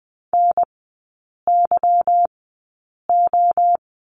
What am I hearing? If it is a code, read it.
Morse code, heard as a keyed tone, about 20 wpm, spaced as for 8 wpm, 690 Hz. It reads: NYO
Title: 159 Lesson 26e - Letter g - Three Characters 20wpm